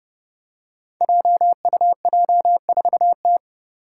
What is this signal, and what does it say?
Morse code, heard as a keyed tone, about 30 wpm, 700 Hz. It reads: JUJ4T
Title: Five Challenging Letter-Number Combinations 30wpm